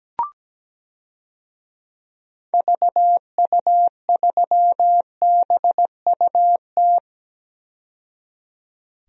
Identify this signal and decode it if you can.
Morse code, heard as a keyed tone, about 17 wpm, 700 Hz. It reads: VU3BUT
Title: DXCC-100 Call Signs - 6 Characters - 17wpm